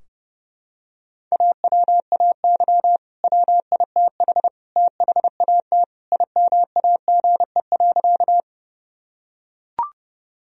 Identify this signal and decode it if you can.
Morse code, heard as a keyed tone, about 30 wpm, 700 Hz. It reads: AWAY WITH THAT IMAGE.
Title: Sentences from Top 1000 Words - Review Part 4 - 30wpm